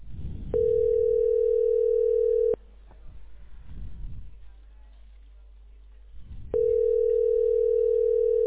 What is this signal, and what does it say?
Ringback tone (North America), 440 + 480 Hz.